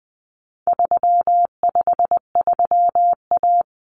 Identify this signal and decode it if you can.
Morse code, heard as a keyed tone, about 20 wpm, 700 Hz. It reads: 353A
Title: Common Cut Numbers - 4-Digit 20wpm